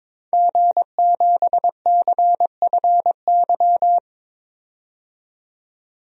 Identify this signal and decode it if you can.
Morse code, heard as a keyed tone, about 21 wpm, 700 Hz. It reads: G7CFY
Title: DXCC-100 Call Signs - 5 Characters - 22wpm